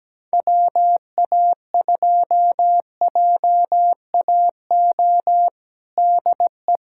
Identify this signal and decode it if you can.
Morse code, heard as a keyed tone, about 17 wpm, 700 Hz. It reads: WA2JAO DE
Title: USA Callsign DE Callsign 17wpm